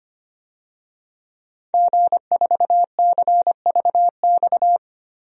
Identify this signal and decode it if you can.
Morse code, heard as a keyed tone, about 25 wpm, 700 Hz. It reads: G4CVX
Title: DXCC-100 Call Signs - 5 Characters - Repeated 2x - 25wpm